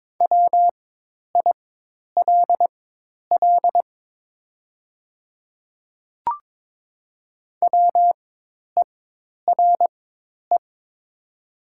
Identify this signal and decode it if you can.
Morse code, heard as a keyed tone, about 22 wpm, spaced as for 10 wpm, 700 Hz. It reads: WILL WERE
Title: 4x Character Spacing - Top 100 Words - 22wpm